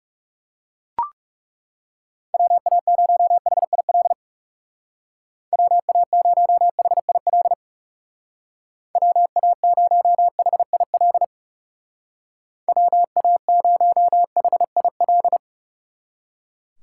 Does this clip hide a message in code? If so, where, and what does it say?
Yes: Morse code from 2.3 to 15.4 s, heard as a keyed tone, about 38 wpm, 700 Hz. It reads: WA0HIL WA0HIL WA0HIL WA0HIL